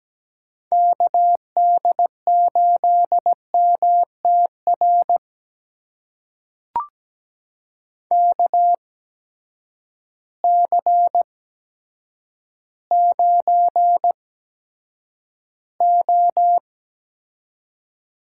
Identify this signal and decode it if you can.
Morse code, heard as a keyed tone, about 17 wpm, spaced as for 8 wpm, 700 Hz. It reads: KD8MTR KC9O
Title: USA Call Signs - 6 Characters - 8x Character Spacing - 17wpm